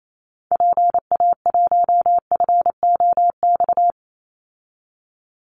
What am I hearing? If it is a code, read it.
Morse code, heard as a keyed tone, about 28 wpm, 700 Hz. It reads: PA1FOX
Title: Call Signs - 28wpm